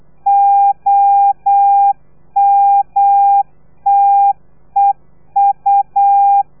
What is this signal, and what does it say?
Morse code, heard as a keyed tone, about 8 wpm, 800 Hz. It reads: OMTEU
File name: telegrafia / i2tfj / Lez3 35